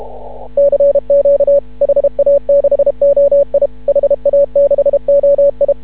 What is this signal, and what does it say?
Morse code, heard as a keyed tone, about 31 wpm, 560 Hz. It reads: CQ HA6OI HA6OI
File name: CW / ha6oi 2